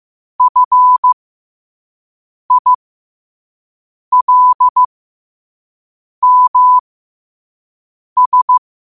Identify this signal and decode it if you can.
Morse code, heard as a keyed tone, about 15 wpm, spaced as for 5 wpm, 1000 Hz. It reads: FILMS
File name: qotdfiles / 5wpm